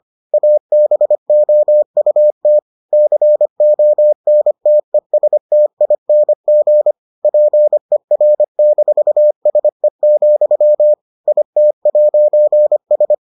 Morse code, heard as a keyed tone, about 25 wpm, 600 Hz. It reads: ABOUT CONTESTING PER-SE, IT'S